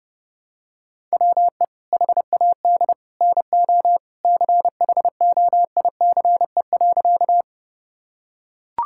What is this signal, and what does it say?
Morse code, heard as a keyed tone, about 30 wpm, 700 Hz. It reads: WE HAD NO CHOICE.
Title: Sentences from Top 1000 Words - Review Part 6 - 30wpm